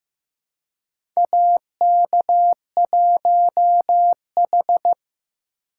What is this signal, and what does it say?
Morse code, heard as a keyed tone, about 15 wpm, 700 Hz. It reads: AK1H
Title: USA Call Signs - 4 Characters - 2x Character Spacing - 15wpm